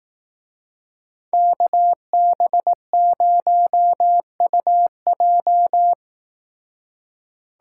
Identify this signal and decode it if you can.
Morse code, heard as a keyed tone, about 18 wpm, 700 Hz. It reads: KB0UJ
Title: Speed Racing - USA Call Signs 15wpm